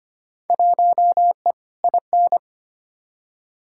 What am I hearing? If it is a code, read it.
Morse code, heard as a keyed tone, about 25 wpm, 700 Hz. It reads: 1E IN
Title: Field Day Exchanges - 25wpm